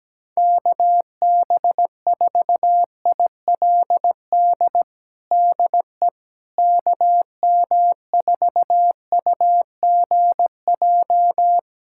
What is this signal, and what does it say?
Morse code, heard as a keyed tone, about 17 wpm, 700 Hz. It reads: KB4ILD DE KM4UGJ
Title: USA Callsign DE Callsign - 6 Characters - 17wpm